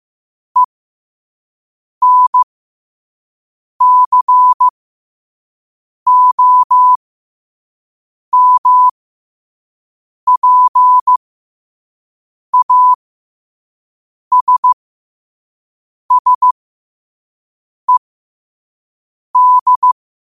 Morse code, heard as a keyed tone, about 15 wpm, spaced as for 5 wpm, 1000 Hz. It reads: ENCOMPASSED
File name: qotdfiles / 5wpm